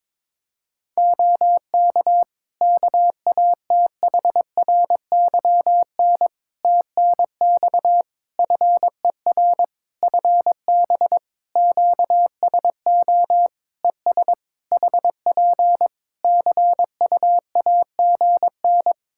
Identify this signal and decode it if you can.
Morse code, heard as a keyed tone, about 22 wpm, 700 Hz. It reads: OK KATHRYN TNX FER FB QSO ES HP CUAGN
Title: K1USN SST QSO Practice 22wpm